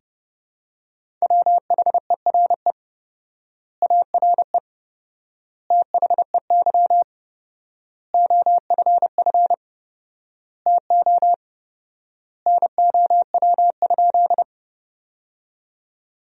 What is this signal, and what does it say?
Morse code, heard as a keyed tone, about 30 wpm, 700 Hz. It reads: WHERE ARE THEY OFF TO NOW?
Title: 4x Word-Spacing - Sentences from Top 200 Words - 30wpm